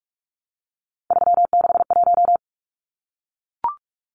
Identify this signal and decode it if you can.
Morse code, heard as a keyed tone, about 44 wpm, 700 Hz. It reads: 361
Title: Numbers - 3-Digit 45wpm